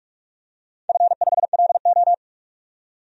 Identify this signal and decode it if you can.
Morse code, heard as a keyed tone, about 44 wpm, 700 Hz. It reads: F5LO
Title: DXCC-100 Call Signs - 4 Characters - 45wpm